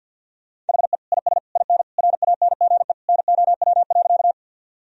Morse code, heard as a keyed tone, about 50 wpm, 700 Hz. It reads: HE IS IN RANGE NOW.
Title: Sentences from Top 1000 Words - Review Part 4 - 50wpm